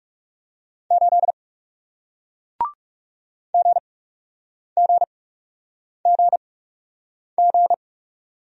Morse code, heard as a keyed tone, about 41 wpm, spaced as for 8 wpm, 700 Hz. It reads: 8 GGGG